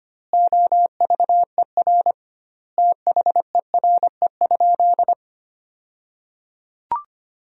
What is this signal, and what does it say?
Morse code, heard as a keyed tone, about 25 wpm, 700 Hz. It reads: OVER THERE?
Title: 2x Word-Spacing - Sentences from Top 200 Words - 25wpm